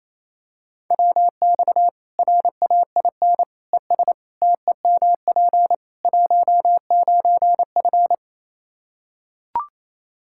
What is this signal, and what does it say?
Morse code, heard as a keyed tone, about 27 wpm, 700 Hz. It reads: WX RAIN ES TEMP 19F
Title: Practice QSO - Exchange 2 - Info - 28wpm